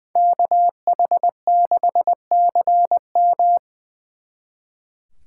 Morse code, heard as a keyed tone, about 23 wpm, 700 Hz. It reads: KH6CM
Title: Speed Racing - USA 2x2 Call Signs 20wpm